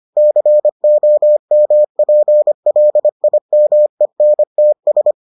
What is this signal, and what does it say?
Morse code, heard as a keyed tone, about 25 wpm, 600 Hz. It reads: COMPLIMENTS